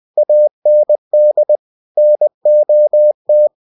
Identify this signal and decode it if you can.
Morse code, heard as a keyed tone, about 20 wpm, 600 Hz. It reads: AND NOT
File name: cw-books / winnie-the-pooh / winnie-20wpm-page-0005